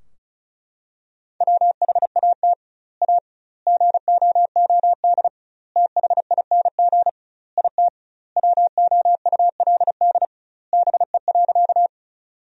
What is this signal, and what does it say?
Morse code, heard as a keyed tone, about 34 wpm, 700 Hz. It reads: WHAT A GOOD THING IT WOULD BE.
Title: 2x Word-Spacing - Sentences from Top 200 Words - 35wpm